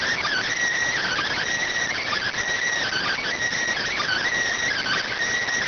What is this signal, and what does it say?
Emergency Alert System (SAME) tones, heard as harsh buzzing bursts. It is heard from 1.0 to 5.1 s.